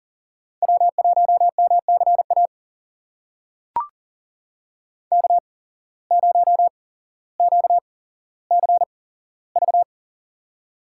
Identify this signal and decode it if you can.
Morse code, heard as a keyed tone, about 40 wpm, spaced as for 17 wpm, 700 Hz. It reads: W1MCA K0QCV
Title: USA Call Signs - 5 Characters - 8x Character Spacing - 40wpm